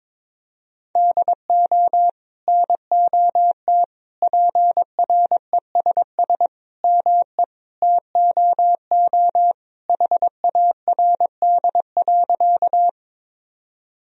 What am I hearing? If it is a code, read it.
Morse code, heard as a keyed tone, about 21 wpm, 700 Hz. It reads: DO NOT PRESS ME TOO HARD.
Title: Sentences from Top 1000 Words - Review Part 1 - 22wpm